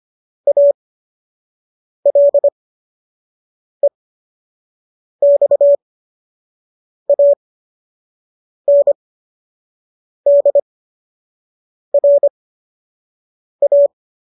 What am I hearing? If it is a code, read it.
Morse code, heard as a keyed tone, about 25 wpm, spaced as for 6 wpm, 570 Hz. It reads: ALEXANDRA
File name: qso012 06